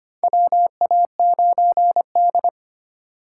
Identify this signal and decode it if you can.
Morse code, heard as a keyed tone, about 25 wpm, 700 Hz. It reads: WA9D
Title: USA Call Signs - 4 Characters - 3x Character Spacing - 25wpm